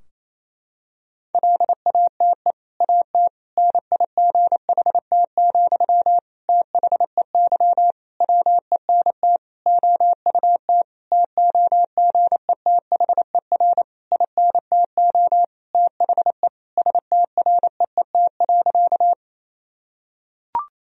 Morse code, heard as a keyed tone, about 27 wpm, 700 Hz. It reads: LATE AT NIGHT, THEY WENT OUT TOGETHER INTO THE STREET.
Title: Sentences from Top 1000 Words - Review Part 3 - 28wpm